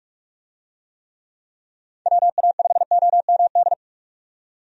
Morse code, heard as a keyed tone, about 43 wpm, 700 Hz. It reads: WA5OMD
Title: USA Call Signs - 6 Characters - 45wpm